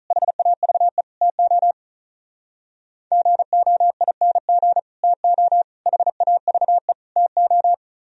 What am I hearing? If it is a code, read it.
Morse code, heard as a keyed tone, about 44 wpm, 700 Hz. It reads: HAVE TO GOING TO HAVE TO